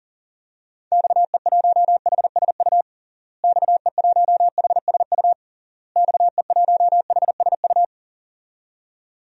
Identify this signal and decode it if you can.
Morse code, heard as a keyed tone, about 37 wpm, 700 Hz. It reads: XE1HSU XE1HSU XE1HSU
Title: DXCC-100 Call Signs - 6 Characters - Repeated 3x - 40wpm